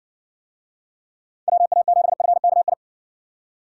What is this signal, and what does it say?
Morse code, heard as a keyed tone, about 60 wpm, 700 Hz. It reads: WA7FKI